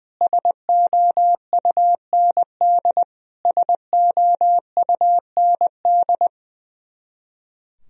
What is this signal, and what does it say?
Morse code, heard as a keyed tone, about 20 wpm, 700 Hz. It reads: SOUND SOUND
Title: Top 100 Words Repeated 20wpm